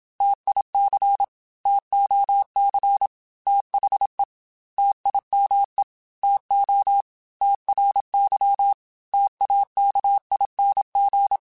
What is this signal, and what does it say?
Morse code, heard as a keyed tone, about 27 wpm, 800 Hz. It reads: TIC TOC THE TIME TO TRY TAKING